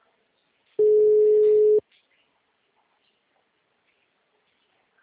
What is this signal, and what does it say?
Ringback tone (Europe, 425 Hz).